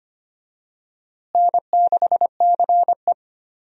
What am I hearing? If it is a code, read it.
Morse code, heard as a keyed tone, about 25 wpm, 700 Hz. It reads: N6CE
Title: USA Call Signs - 4 Characters - Repeated 3x - 25wpm